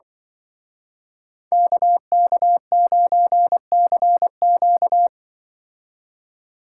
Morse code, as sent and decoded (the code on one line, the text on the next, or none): -.- -.- ----. -.-. --.-
KK9CQ